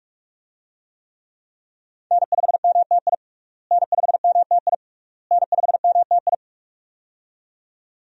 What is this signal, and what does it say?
Morse code, heard as a keyed tone, about 47 wpm, 700 Hz. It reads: N5MTI N5MTI N5MTI
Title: USA Call Signs - 5 Characters - Repeated 3x - 45wpm